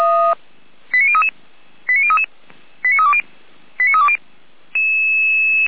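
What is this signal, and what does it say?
Five-tone selective call (ZVEI), heard as quick runs of five tones, tones of about 70 ms. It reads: 89033 89033 89320 89320
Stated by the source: ZVEI five-tone selective call received on an SDR.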